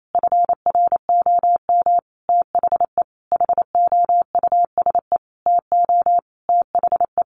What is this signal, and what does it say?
Morse code, heard as a keyed tone, about 28 wpm, 700 Hz. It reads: FROM THE HOUSE TO THE
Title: Sentences from Top 1200 Words 28wpm